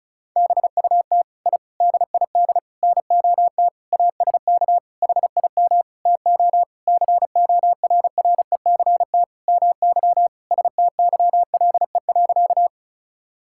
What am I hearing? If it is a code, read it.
Morse code, heard as a keyed tone, about 34 wpm, 700 Hz. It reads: BUT I DID NOT ASK HIM TO CORRECT MY STYLE.
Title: Sentences from Top 1000 Words - Review Part 5 - 35wpm